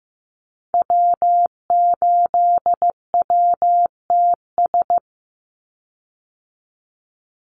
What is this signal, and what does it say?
Morse code, heard as a keyed tone, about 15 wpm, 700 Hz. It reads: W8WTS
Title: Call Signs Repeated 2x - 15wpm